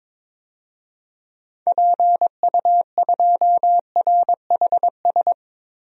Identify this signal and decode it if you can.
Morse code, heard as a keyed tone, about 22 wpm, 700 Hz. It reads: PU2RHS
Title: DXCC-100 Call Signs - 6 Characters - 22wpm